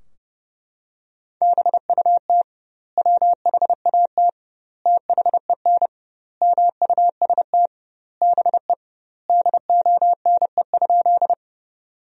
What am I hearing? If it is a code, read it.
Morse code, heard as a keyed tone, about 30 wpm, 700 Hz. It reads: BUT WHAT THEN MUST BE DONE?
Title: 2x Word-Spacing - Sentences from Top 500 Words - 30wpm